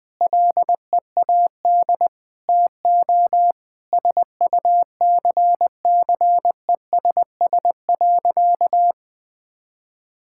Morse code, heard as a keyed tone, about 21 wpm, 700 Hz. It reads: LEAD TO SUCCESS.
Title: Sentences from Top 1000 Words - Review Part 6 - 20wpm